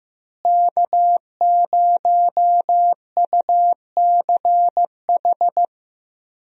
Morse code, heard as a keyed tone, about 15 wpm, 700 Hz. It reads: K0UCH
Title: USA Call Signs - 5 Characters - Repeated 3x - 15wpm